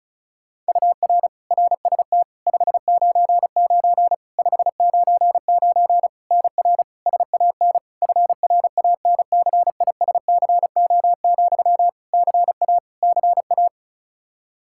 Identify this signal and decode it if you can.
Morse code, heard as a keyed tone, about 36 wpm, 700 Hz. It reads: UR RST 599 599 NR SAN FRANCISCO, CA CA
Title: Practice QSO - Exchange 1 - Beginning - 35wpm